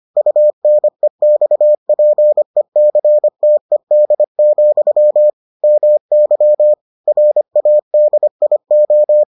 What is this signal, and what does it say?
Morse code, heard as a keyed tone, about 25 wpm, 600 Hz. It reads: UNEXPECTED, MY RADIO